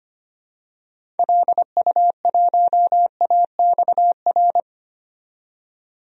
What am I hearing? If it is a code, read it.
Morse code, heard as a keyed tone, about 25 wpm, 700 Hz. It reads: LU1AXR
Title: DXCC-100 Call Signs - 6 Characters - 25wpm